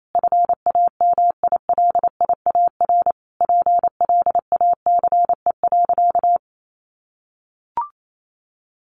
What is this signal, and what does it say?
Morse code, heard as a keyed tone, about 28 wpm, 700 Hz. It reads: FAMILIAR PLACE.
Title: Sentences from Top 1500 Words 28wpm